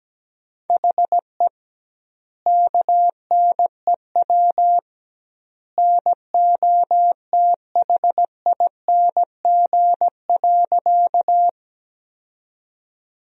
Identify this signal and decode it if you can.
Morse code, heard as a keyed tone, about 17 wpm, 700 Hz. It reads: HE KNEW NOTHING.
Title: 2x Word-Spacing - Sentences from Top 500 Words - 17wpm